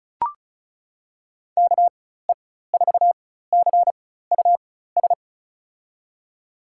Morse code, heard as a keyed tone, about 34 wpm, spaced as for 16 wpm, 700 Hz. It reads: KE4CUS
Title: USA Call Signs - 4x Character Spacing - 35wpm